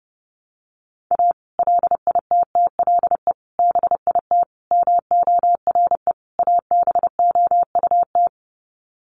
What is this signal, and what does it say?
Morse code, heard as a keyed tone, about 30 wpm, 700 Hz. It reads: A LITTLE BIT MORE ABOUT
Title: Sets of 5 Words Encore2 30wpm